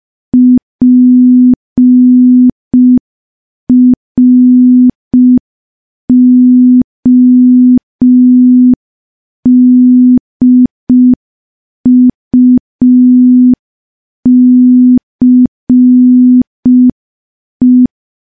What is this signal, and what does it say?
Morse code, heard as a keyed tone, about 5 wpm, 250 Hz. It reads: PRODUCE
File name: morse / morse-training-05WPM